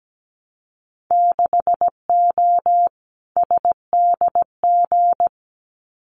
Morse code, heard as a keyed tone, about 17 wpm, 700 Hz. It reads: 6O SDG
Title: Winter Field Day Exchanges Encore 17wpm